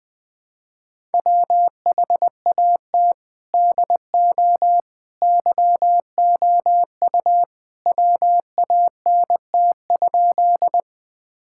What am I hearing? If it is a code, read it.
Morse code, heard as a keyed tone, about 20 wpm, 700 Hz. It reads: WHAT DO YOU WANT?
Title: Sentences with 2-Word Phrases - 20wpm